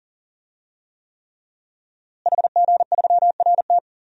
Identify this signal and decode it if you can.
Morse code, heard as a keyed tone, about 40 wpm, 700 Hz. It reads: HG3RT
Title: DXCC-100 Call Signs - 5 Characters - Repeated 3x - 40wpm